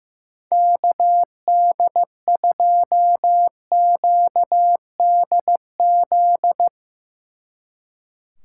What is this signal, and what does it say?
Morse code, heard as a keyed tone, about 15 wpm, 700 Hz. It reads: KD2QDZ
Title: Speed Racing - USA 2x3 Call Signs 15wpm